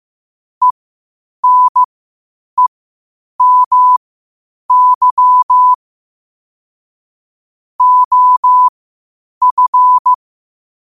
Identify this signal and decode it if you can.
Morse code, heard as a keyed tone, about 15 wpm, spaced as for 8 wpm, 1000 Hz. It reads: ENEMY OF